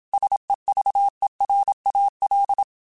Morse code, heard as a keyed tone, about 27 wpm, 800 Hz. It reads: SEVERAL